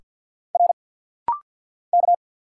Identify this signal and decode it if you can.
Morse code, heard as a keyed tone, about 50 wpm, 700 Hz. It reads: R K
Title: Single Letters - Rapid-Fire 50wpm